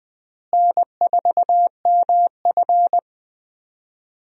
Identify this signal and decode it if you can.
Morse code, heard as a keyed tone, about 20 wpm, 700 Hz. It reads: N4MF
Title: USA Call Signs - 4 Characters - Repeated 3x - 20wpm